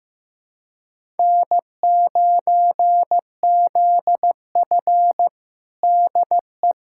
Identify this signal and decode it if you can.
Morse code, heard as a keyed tone, about 15 wpm, 700 Hz. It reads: N9ZF DE
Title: USA Callsign DE Callsign - 4 Characters - 15wpm